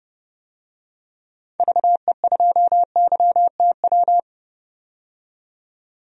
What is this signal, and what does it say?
Morse code, heard as a keyed tone, about 29 wpm, 700 Hz. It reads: VE2YTW
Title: DXCC-100 Call Signs - 6 Characters - 30wpm